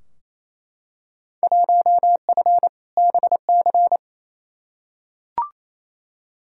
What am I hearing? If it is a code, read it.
Morse code, heard as a keyed tone, about 28 wpm, 700 Hz. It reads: 1F BC
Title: Field Day Exchanges Encore - 28wpm